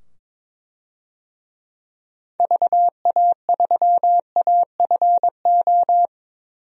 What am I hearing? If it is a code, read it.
Morse code, heard as a keyed tone, about 22 wpm, 700 Hz. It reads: VA3AFO
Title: DXCC-100 Call Signs - 6 Characters - 22wpm